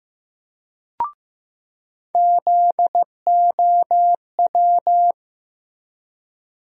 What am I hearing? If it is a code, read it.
Morse code, heard as a keyed tone, about 16 wpm, 700 Hz. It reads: ZOW
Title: USA Call Sign Postfixes - 3 Letters 15wpm